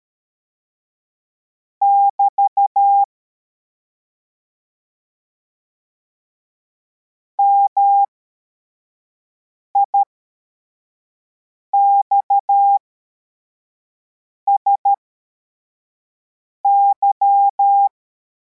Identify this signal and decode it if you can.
Morse code, heard as a keyed tone, about 13 wpm, spaced as for 4 wpm, 800 Hz. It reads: <BT> MIXSY
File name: formation / cw / Text62